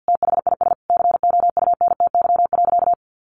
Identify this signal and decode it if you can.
Morse code, heard as a keyed tone, about 49 wpm, 700 Hz. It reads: THIS COUNTY.
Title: Sentences from Top 1200 Words 50wpm